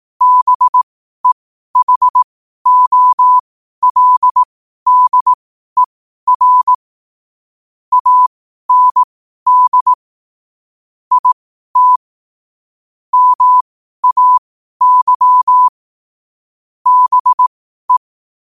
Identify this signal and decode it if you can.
Morse code, heard as a keyed tone, about 18 wpm, spaced as for 12 wpm, 1000 Hz. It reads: BEHOLDER AND IT MAY BE
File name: qotdfiles / 12wpm